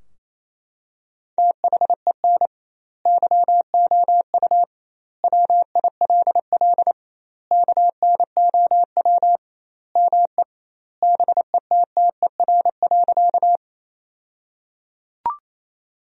Morse code, heard as a keyed tone, about 28 wpm, 700 Hz. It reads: THEN YOU WILL KNOW ME BETTER.
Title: 2x Word-Spacing - Sentences from Top 400 Words - 28wpm